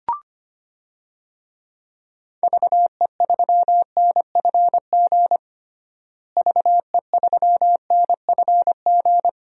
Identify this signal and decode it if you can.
Morse code, heard as a keyed tone, about 25 wpm, 700 Hz. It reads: VE3NFG VE3NFG
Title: DXCC-100 Call Signs - 6 Characters - Repeated 2x - 25wpm